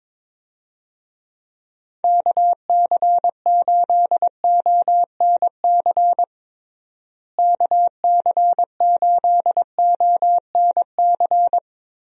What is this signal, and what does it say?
Morse code, heard as a keyed tone, about 22 wpm, 700 Hz. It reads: KC8ONC KC8ONC
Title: USA Call Signs - 6 Characters - Repeated 3x - 22wpm